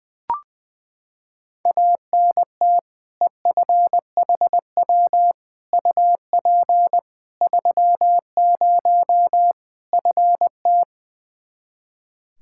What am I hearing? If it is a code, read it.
Morse code, heard as a keyed tone, about 21 wpm, 700 Hz. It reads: ANT EFHW UP 30 FT
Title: Practice QSO - Exchange 2 - Info - 20wpm